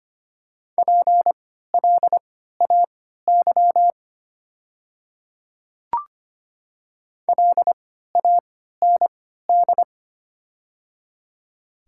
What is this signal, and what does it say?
Morse code, heard as a keyed tone, about 25 wpm, spaced as for 14 wpm, 690 Hz. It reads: PLAY LAND
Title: 3x Character Spacing - Top 300 Words - 25wpm